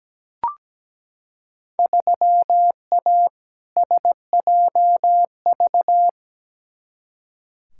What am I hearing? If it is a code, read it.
Morse code, heard as a keyed tone, about 17 wpm, 700 Hz. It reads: 3A SJV
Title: Field Day Exchanges Encore - 17wpm